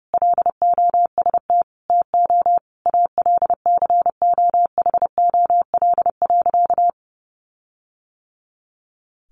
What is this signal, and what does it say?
Morse code, heard as a keyed tone, about 30 wpm, 700 Hz. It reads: LOST TO ALCOHOL.
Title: Sentences from Top 1400 Words 30wpm